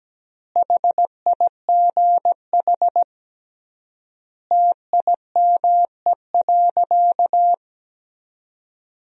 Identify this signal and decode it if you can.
Morse code, heard as a keyed tone, about 17 wpm, 700 Hz. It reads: HIGH TIME.
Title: 3x Word-Spacing - Sentences from Top 200 Words - 17wpm